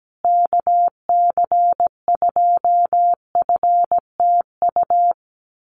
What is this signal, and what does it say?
Morse code, heard as a keyed tone, about 17 wpm, 700 Hz. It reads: KC2FTU
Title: USA Call Signs Repeated 2x 17wpm